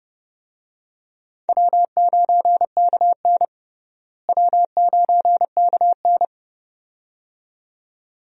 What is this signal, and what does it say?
Morse code, heard as a keyed tone, about 32 wpm, 700 Hz. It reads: W9KN W9KN
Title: USA Call Signs - 4 Characters - Repeated 2x - 30wpm